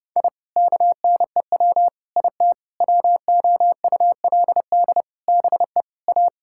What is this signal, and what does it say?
Morse code, heard as a keyed tone, about 30 wpm, 700 Hz. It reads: I KNEW IT WOULD BE A
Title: Sentences from Top 1000 Words - Review Part 6 - 30wpm